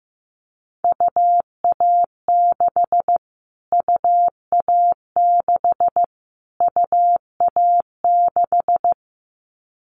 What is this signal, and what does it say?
Morse code, heard as a keyed tone, about 15 wpm, 700 Hz. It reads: UA6 UA6 UA6
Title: Callsign Prefixes for Top 25 DXCC - Repeated 3x - 15wpm